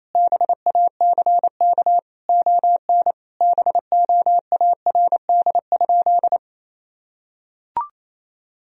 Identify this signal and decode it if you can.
Morse code, heard as a keyed tone, about 27 wpm, 700 Hz. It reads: BACK ON BOARD?
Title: Sentences from Top 1000 Words - Review Part 4 - 28wpm